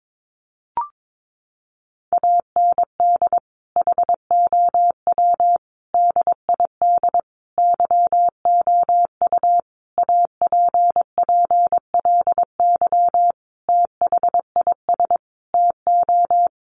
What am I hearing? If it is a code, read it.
Morse code, heard as a keyed tone, about 22 wpm, 700 Hz. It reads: AND HOW DID YOU APPLY THIS TO
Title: Sentences from Top 1200 Words 22wpm